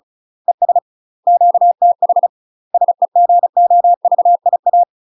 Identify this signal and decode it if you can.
Morse code, heard as a keyed tone, about 36 wpm, 700 Hz. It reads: ES QTH SEGOVIA
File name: UK MASTER QSOs TRAINING / 35WPM / UK MASTER QSOs 0011